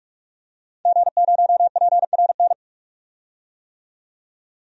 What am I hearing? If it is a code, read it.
Morse code, heard as a keyed tone, about 44 wpm, 700 Hz. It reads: G0PRN
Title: DXCC-100 Call Signs - 5 Characters - 45wpm